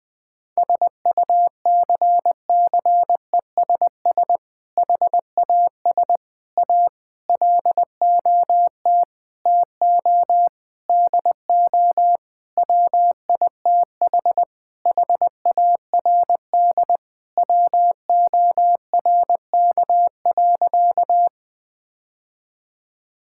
Morse code, heard as a keyed tone, about 20 wpm, 700 Hz. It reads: SUCCESS HAS A LOT TO DO WITH HARD WORK.
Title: Sentences with 6-Word Phrases - 20wpm